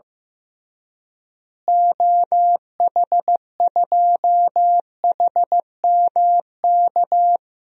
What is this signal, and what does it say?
Morse code, heard as a keyed tone, about 15 wpm, 700 Hz. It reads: OH2HMK
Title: DXCC-100 Call Signs - 6 Characters - Repeated 3x - 15wpm